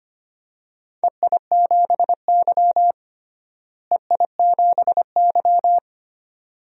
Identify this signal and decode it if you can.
Morse code, heard as a keyed tone, about 25 wpm, 700 Hz. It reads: EI7Y EI7Y
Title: DXCC-100 Call Signs - 4 Characters - Repeated 2x - 25wpm